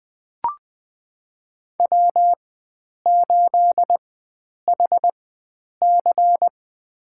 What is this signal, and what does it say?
Morse code, heard as a keyed tone, about 21 wpm, spaced as for 9 wpm, 700 Hz. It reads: W8HC
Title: USA Call Signs - 4x Character Spacing - 20wpm